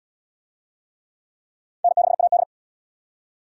Morse code, heard as a keyed tone, about 65 wpm, 700 Hz. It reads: N6AD